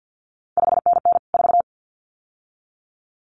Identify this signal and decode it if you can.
Morse code, heard as a keyed tone, about 50 wpm, 700 Hz. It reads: 5NN 4
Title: CQ WW DX Contest Exchange - 50wpm